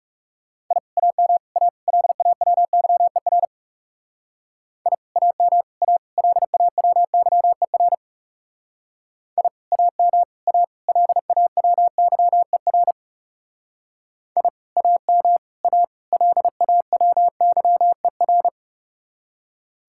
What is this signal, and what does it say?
Morse code, heard as a keyed tone, about 39 wpm, 700 Hz. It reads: I AM A LAWYER I AM A LAWYER I AM A LAWYER I AM A LAWYER